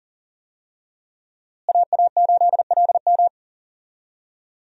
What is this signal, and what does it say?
Morse code, heard as a keyed tone, about 37 wpm, 700 Hz. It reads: AA8LM
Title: USA Call Signs - 5 Characters - Repeated 2x - 40wpm